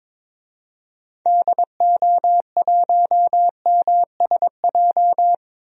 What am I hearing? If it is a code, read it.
Morse code, heard as a keyed tone, about 22 wpm, 700 Hz. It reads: DO1MSJ
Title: DXCC-100 Call Signs - 6 Characters - 22wpm